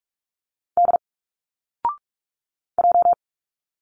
Morse code, heard as a keyed tone, about 45 wpm, 700 Hz. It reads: D J
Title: Single Letters 45wpm